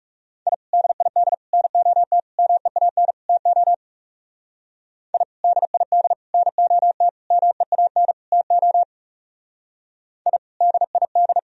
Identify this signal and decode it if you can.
Morse code, heard as a keyed tone, about 37 wpm, 700 Hz. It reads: I DID NOT MEAN TO I DID NOT MEAN TO I DID
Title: Speed-Racing - Sets of 5 Words 30wpm